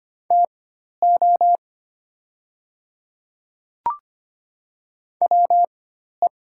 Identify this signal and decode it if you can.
Morse code, heard as a keyed tone, about 25 wpm, spaced as for 12 wpm, 690 Hz. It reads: TO WE
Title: 4x Character Spacing - Top 300 Words - 25wpm